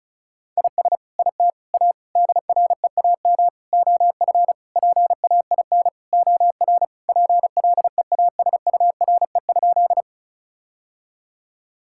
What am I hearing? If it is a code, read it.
Morse code, heard as a keyed tone, about 36 wpm, 700 Hz. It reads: IS IT A DREAM OF PAIN OR PLEASURE?
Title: Sentences from Top 1600 Words 35wpm